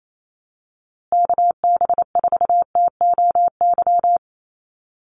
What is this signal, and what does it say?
Morse code, heard as a keyed tone, about 28 wpm, 700 Hz. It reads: KB4TOY
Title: USA Call Signs Repeated 2x 28wpm